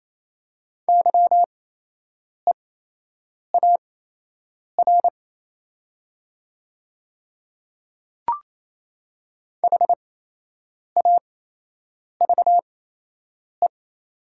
Morse code, heard as a keyed tone, about 28 wpm, spaced as for 8 wpm, 700 Hz. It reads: YEAR HAVE
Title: 8x Character Spacing - Top 200 Words - 28wpm